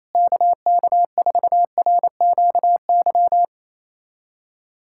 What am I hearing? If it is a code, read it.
Morse code, heard as a keyed tone, about 28 wpm, 700 Hz. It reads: KK4RQY
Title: USA Callsign DE Callsign 28wpm